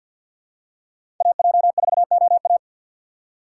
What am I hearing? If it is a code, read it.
Morse code, heard as a keyed tone, about 51 wpm, 700 Hz. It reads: AJ4OA
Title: USA Call Signs - 5 Characters - 4x Character Spacing - 50wpm